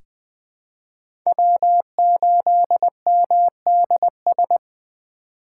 Morse code, heard as a keyed tone, about 21 wpm, 700 Hz. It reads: W8MDS
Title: USA Call Signs - 5 Characters - 3x Character Spacing - 20wpm